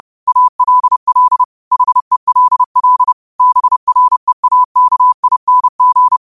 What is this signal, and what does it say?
Morse code, heard as a keyed tone, about 30 wpm, 1000 Hz. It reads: ALL HELL BREAKING
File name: qotdfiles / 30wpm